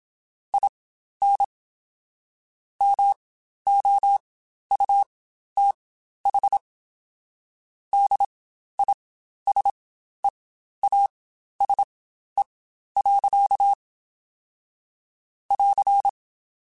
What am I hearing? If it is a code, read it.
Morse code, heard as a keyed tone, about 27 wpm, spaced as for 12 wpm, 800 Hz. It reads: IN MOUTH DISEASE. <AR>
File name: Quotes Spaced / Quotes Spaced-mp3s / HoofInMouth sp @25